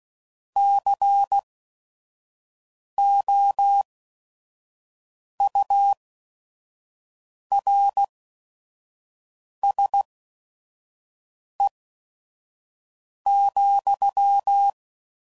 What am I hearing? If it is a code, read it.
Morse code, heard as a keyed tone, about 16 wpm, spaced as for 5 wpm, 800 Hz. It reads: COURSE,
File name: CW Code Groups @5WPM / QSOs / QSO WA7QOX 1